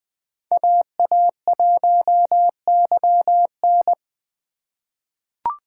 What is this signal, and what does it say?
Morse code, heard as a keyed tone, about 20 wpm, 700 Hz. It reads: AA1YN
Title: USA Call Signs - 5 Characters - 4x Character Spacing - 20wpm